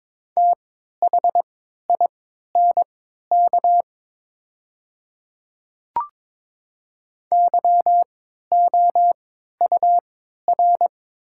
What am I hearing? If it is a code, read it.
Morse code, heard as a keyed tone, about 22 wpm, spaced as for 12 wpm, 700 Hz. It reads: THINK YOUR
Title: 3x Character Spacing - Top 200 Words - 22wpm